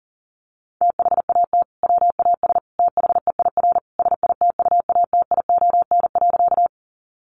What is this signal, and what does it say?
Morse code, heard as a keyed tone, about 40 wpm, 700 Hz. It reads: THAT WAS THEIR SITUATION.
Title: Sentences from Top 1200 Words 40wpm